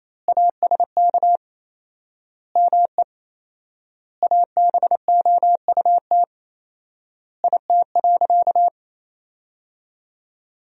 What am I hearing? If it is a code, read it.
Morse code, heard as a keyed tone, about 27 wpm, 700 Hz. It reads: ASK ME ABOUT IT.
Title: 4x Word-Spacing - Sentences from Top 200 Words - 28wpm